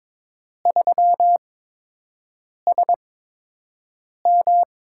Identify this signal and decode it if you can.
Morse code, heard as a keyed tone, about 22 wpm, spaced as for 6 wpm, 700 Hz. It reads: 3SM
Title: USA Call Signs - 4 Characters - 8x Character Spacing - 22wpm